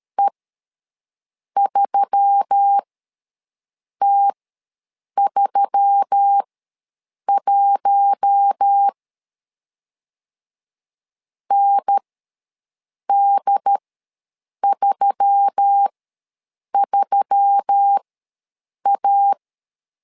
Morse code, heard as a keyed tone, about 13 wpm, spaced as for 6 wpm, 800 Hz. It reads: E3T31 ND33A
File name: LEZIONI / 4 Q1S3 ws 12wpm 2038